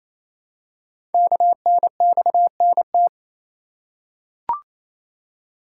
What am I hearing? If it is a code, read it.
Morse code, heard as a keyed tone, about 29 wpm, 700 Hz. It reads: KNXNT
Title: Code Groups - 5 Letters - 28wpm